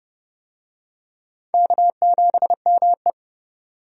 Morse code, heard as a keyed tone, about 29 wpm, 700 Hz. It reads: K7ME
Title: USA Call Signs - 4 Characters - 30wpm